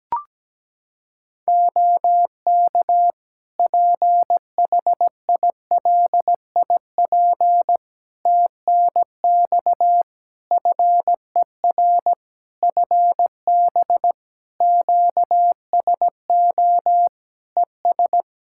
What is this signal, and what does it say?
Morse code, heard as a keyed tone, about 17 wpm, 700 Hz. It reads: OK PHILIP TNX FER FB QSO ES
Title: K1USN SST QSO Practice 17wpm